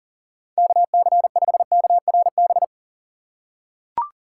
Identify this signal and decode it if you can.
Morse code, heard as a keyed tone, about 39 wpm, 700 Hz. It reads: KC5KRB
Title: USA Call Signs - 6 Characters - 2x Character Spacing - 40wpm